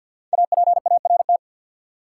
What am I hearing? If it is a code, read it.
Morse code, heard as a keyed tone, about 53 wpm, 700 Hz. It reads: APART
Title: Five Letter Words - 50wpm